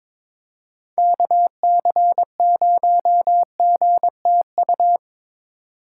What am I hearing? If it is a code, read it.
Morse code, heard as a keyed tone, about 23 wpm, 700 Hz. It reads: KC0GTU
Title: USA Call Signs - 6x Character Spacing - 22wpm